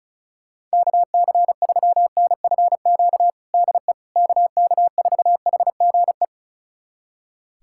Morse code, heard as a keyed tone, about 36 wpm, 700 Hz. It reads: KC3NFQ DE KK4HGE
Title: USA Callsign DE Callsign - 6 Characters - 35wpm